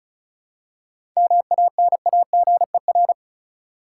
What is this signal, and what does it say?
Morse code, heard as a keyed tone, about 36 wpm, 700 Hz. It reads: MANAGER
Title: Seven Letter Words - 35wpm